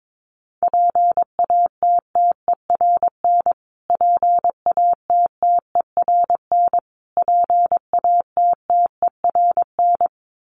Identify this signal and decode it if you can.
Morse code, heard as a keyed tone, about 22 wpm, 700 Hz. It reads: PATTERN PATTERN PATTERN
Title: Top 500 Words Repeated 22wpm